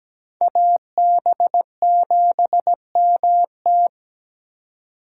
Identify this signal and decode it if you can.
Morse code, heard as a keyed tone, about 20 wpm, 700 Hz. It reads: AB7MT
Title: Speed Racing - USA Call Signs 17wpm